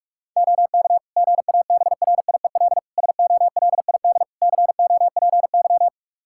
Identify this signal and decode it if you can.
Morse code, heard as a keyed tone, about 46 wpm, 700 Hz. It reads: OK GABRIEL SOLID COPY
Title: Practice QSO - Exchange 3 - More Info - 45wpm